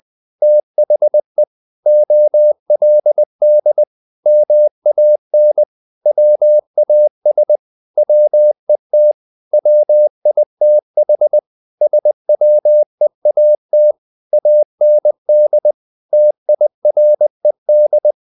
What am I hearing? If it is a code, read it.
Morse code, heard as a keyed tone, about 20 wpm, 600 Hz. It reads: THE OLD MAN WAS WET WITH SWEAT AND TIRED
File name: the old man and the sea part 2 20wpm